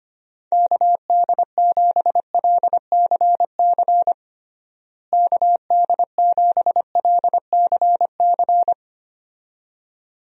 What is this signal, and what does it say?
Morse code, heard as a keyed tone, about 25 wpm, 700 Hz. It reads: KD7LCC KD7LCC
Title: USA Call Signs - 6 Characters - Repeated 2x - 25wpm